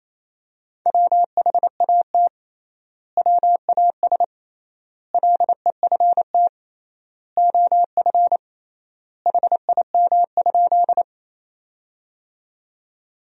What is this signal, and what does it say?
Morse code, heard as a keyed tone, about 28 wpm, 700 Hz. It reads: WHAT WAS LEFT OF HIM?
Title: 3x Word-Spacing - Sentences from Top 300 Words - 28wpm